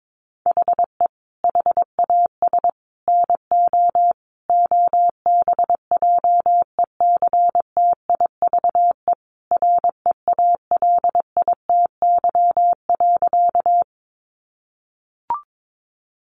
Morse code, heard as a keyed tone, about 22 wpm, 700 Hz. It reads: HE HAS NO OBJECTIVE REALITY.
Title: Sentences from Top 1200 Words 22wpm